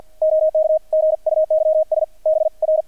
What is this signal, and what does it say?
Morse code, heard as a keyed tone, about 48 wpm, 630 Hz, with radio noise. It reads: OK GUYS BR
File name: archived / CRSnet